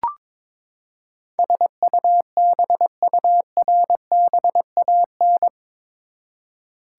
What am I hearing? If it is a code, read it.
Morse code, heard as a keyed tone, about 22 wpm, 700 Hz. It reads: SUBURBAN
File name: practice / 22wpm / Eight Letter Words - 22wpm